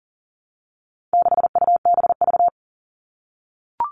Morse code, heard as a keyed tone, about 40 wpm, 700 Hz. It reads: 6UBV
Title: All Cut Numbers - 4-Digit 40wpm